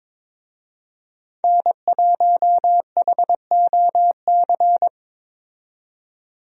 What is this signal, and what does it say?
Morse code, heard as a keyed tone, about 22 wpm, 700 Hz. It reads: N1HOC
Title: USA Call Signs - 5 Characters - 22wpm